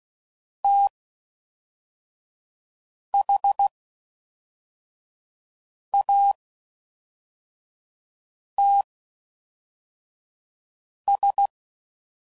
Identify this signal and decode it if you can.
Morse code, heard as a keyed tone, about 16 wpm, spaced as for 4 wpm, 800 Hz. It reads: THATS